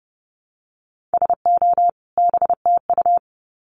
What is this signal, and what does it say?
Morse code, heard as a keyed tone, about 30 wpm, 700 Hz. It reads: SO BTU
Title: Common QSO Phrases 30wpm